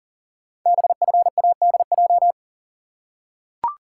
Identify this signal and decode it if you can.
Morse code, heard as a keyed tone, about 40 wpm, 700 Hz. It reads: BFADJ
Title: Code Groups - 5 Letters - 40wpm